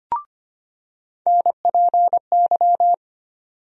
Morse code, heard as a keyed tone, about 25 wpm, 700 Hz. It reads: NPY
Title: USA Call Sign Postfixes - 3 Letters 25wpm